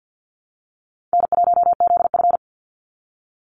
Morse code, heard as a keyed tone, about 50 wpm, 700 Hz. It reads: N1ZF
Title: NA SOTA Activator Callsigns 50wpm